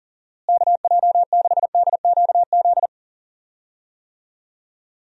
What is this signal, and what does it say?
Morse code, heard as a keyed tone, about 38 wpm, 700 Hz. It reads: KJ6DQZ
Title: USA Call Signs - 6 Characters - Repeated 2x - 40wpm